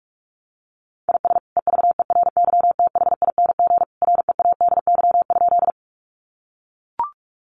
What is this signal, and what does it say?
Morse code, heard as a keyed tone, about 44 wpm, 700 Hz. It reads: IS EVERYTHING READY?
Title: Sentences from Top 700 Words 45wpm